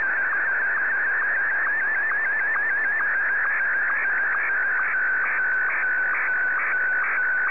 SSTV (slow-scan television), Martin 1.